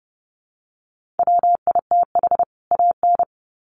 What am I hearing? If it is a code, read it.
Morse code, heard as a keyed tone, about 30 wpm, 700 Hz. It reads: WITH AN
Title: Sets of 2 Words 30wpm